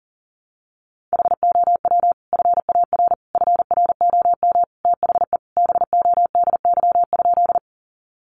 Morse code, heard as a keyed tone, about 40 wpm, 700 Hz. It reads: HOW FAR FROM THE BODY?
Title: Sentences from Top 400 Words 40wpm